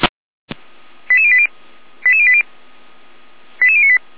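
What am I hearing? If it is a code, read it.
Five-tone selective call (ZVEI), heard as quick runs of five tones, tones of about 70 ms. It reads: 88080 88080 88098
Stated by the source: Five-tone ZVEI selective call 88571 received by SDR.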